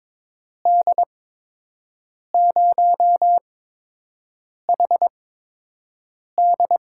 Morse code, heard as a keyed tone, about 21 wpm, spaced as for 6 wpm, 700 Hz. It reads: D0HD
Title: USA Call Signs - 5 Characters - 8x Character Spacing - 22wpm